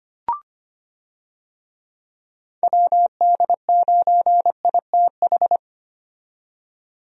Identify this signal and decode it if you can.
Morse code, heard as a keyed tone, about 25 wpm, 700 Hz. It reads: WD9ITH
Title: USA Call Signs - 6 Characters - 25wpm